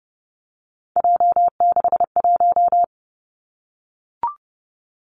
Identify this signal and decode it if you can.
Morse code, heard as a keyed tone, about 30 wpm, 700 Hz. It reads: J61
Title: Three Challenging Letter-Number Combinations 30wpm